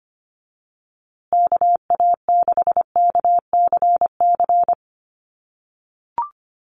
Morse code, heard as a keyed tone, about 25 wpm, 700 Hz. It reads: KA6KCC
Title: USA Call Signs Repeated 2x 25wpm